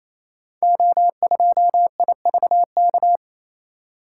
Morse code, heard as a keyed tone, about 29 wpm, 700 Hz. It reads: O2IVK
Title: Code Groups - 5 Letter:Numbers - 28wpm